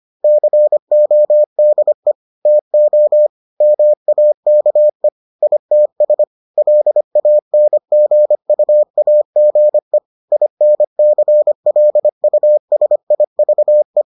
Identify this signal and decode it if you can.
Morse code, heard as a keyed tone, about 25 wpm, 600 Hz. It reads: CODE TO MAKE ITS LANGUAGE INCLUSIVE